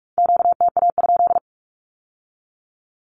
Morse code, heard as a keyed tone, about 46 wpm, 700 Hz. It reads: QTA?
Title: Q-Codes with Spoken Meaning - 45wpm